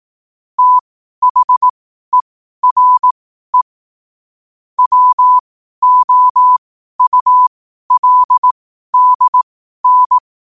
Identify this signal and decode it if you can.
Morse code, heard as a keyed tone, about 18 wpm, spaced as for 12 wpm, 1000 Hz. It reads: THERE WOULDN
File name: qotdfiles / 12wpm